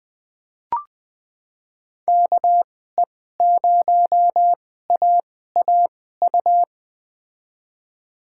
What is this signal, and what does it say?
Morse code, heard as a keyed tone, about 21 wpm, spaced as for 15 wpm, 700 Hz. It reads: KE0AAU
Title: USA Call Signs - 6 Characters - 2x Character Spacing - 20wpm